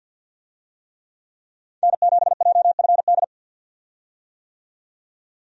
Morse code, heard as a keyed tone, about 50 wpm, 700 Hz. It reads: N7JFD
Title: USA Call Signs - 5 Characters - 50wpm